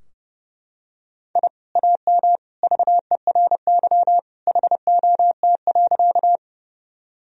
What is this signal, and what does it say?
Morse code, heard as a keyed tone, about 29 wpm, 700 Hz. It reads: I AM VERY HOT.
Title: Sentences from Top 1000 Words - Review Part 3 - 30wpm